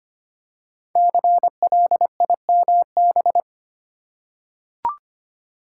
Morse code, heard as a keyed tone, about 25 wpm, 700 Hz. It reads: CLIMB
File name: 5 Letter Words - First 3 Letters Given - 25wpm